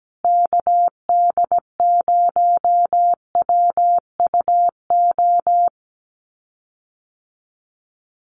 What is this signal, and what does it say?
Morse code, heard as a keyed tone, about 17 wpm, 700 Hz. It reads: KD0WUO
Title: USA Call Signs Repeated 3x 17wpm